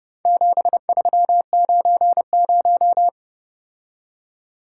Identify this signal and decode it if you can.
Morse code, heard as a keyed tone, about 30 wpm, 700 Hz. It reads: 7390